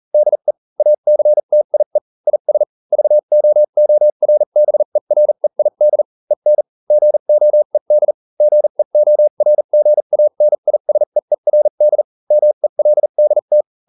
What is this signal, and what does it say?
Morse code, heard as a keyed tone, about 40 wpm, spaced as for 35 wpm, 600 Hz. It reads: DE ACTIE IS VOORBEREID EN GOED GEORGANISEERD MELDT